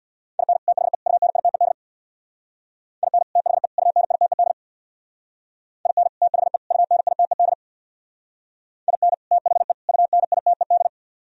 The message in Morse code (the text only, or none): IN THE UNITED IN THE UNITED IN THE UNITED IN THE UNITED